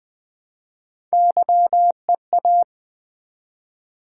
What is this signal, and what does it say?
Morse code, heard as a keyed tone, about 20 wpm, 700 Hz. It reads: YEA
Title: Build A Word - 1 Letter At A Time - 4-Letter Words - 20wpm